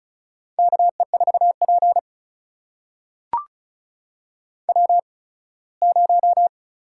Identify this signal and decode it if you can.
Morse code, heard as a keyed tone, about 36 wpm, spaced as for 21 wpm, 700 Hz. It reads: KE4P W0
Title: USA Call Signs - 4 Characters - 8x Character Spacing - 35wpm